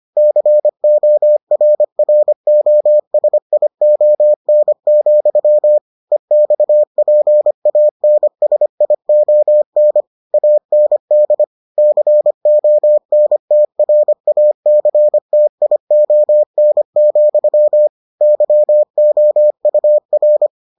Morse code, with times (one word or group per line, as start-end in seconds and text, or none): 0.2-5.8 CORROSION,
6.1-10.0 EXPANSION
10.3-11.4 AND
11.8-17.9 CONTRACTION,
18.2-20.5 YOUR